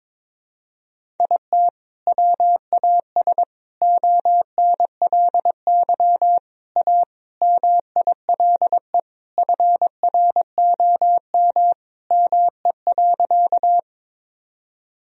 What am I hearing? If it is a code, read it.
Morse code, heard as a keyed tone, about 22 wpm, 700 Hz. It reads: IT WAS ONLY A MILE FROM ME.
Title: Sentences from Top 1000 Words - Review Part 1 - 22wpm